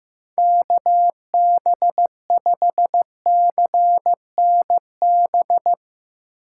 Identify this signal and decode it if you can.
Morse code, heard as a keyed tone, about 15 wpm, 700 Hz. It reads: KB5CNB
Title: Speed Racing - USA 2x3 Call Signs 15wpm